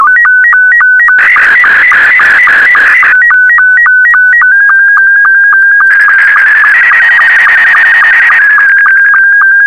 SSTV (slow-scan television), Scottie 2.